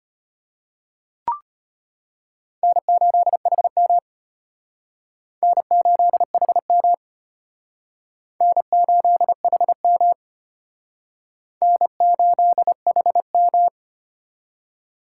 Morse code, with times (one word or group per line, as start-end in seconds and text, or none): 2.6-4.0 N8HM
5.4-6.9 N8HM
8.4-10.1 N8HM
11.6-13.7 N8HM